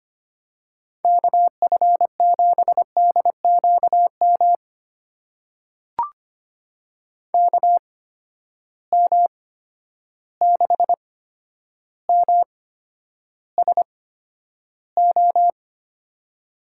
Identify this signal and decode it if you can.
Morse code, heard as a keyed tone, about 25 wpm, spaced as for 11 wpm, 700 Hz. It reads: KF7DQM KM6MSO
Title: USA Call Signs - 6 Characters - 8x Character Spacing - 25wpm